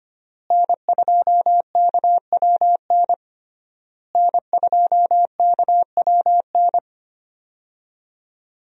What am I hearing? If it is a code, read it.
Morse code, heard as a keyed tone, about 25 wpm, 700 Hz. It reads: N2KWN N2KWN
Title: USA Call Signs - 5 Characters - Repeated 3x - 25wpm